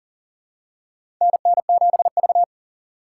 Morse code, heard as a keyed tone, about 40 wpm, 700 Hz. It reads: NN7V
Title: USA Call Signs - 4 Characters - 3x Character Spacing - 40wpm